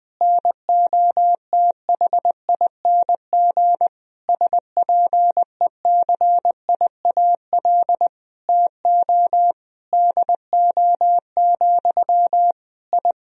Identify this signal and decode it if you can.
Morse code, heard as a keyed tone, about 20 wpm, 700 Hz. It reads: NOTHING SPECIAL TO DO, I
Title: Sentences from Top 1000 Words - Review Part 3 - 20wpm